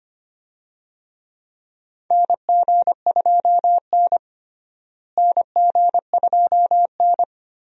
Morse code, heard as a keyed tone, about 25 wpm, 700 Hz. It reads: NG2N NG2N
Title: USA Call Signs - 4 Characters - Repeated 2x - 25wpm